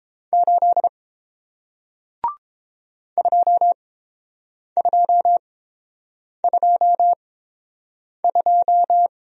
Morse code, heard as a keyed tone, about 30 wpm, spaced as for 7 wpm, 700 Hz. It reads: 8 2222